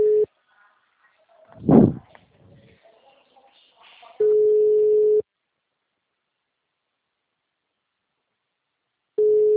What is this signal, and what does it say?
Ringback tone (Europe, 425 Hz).
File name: out- -2039- - - .2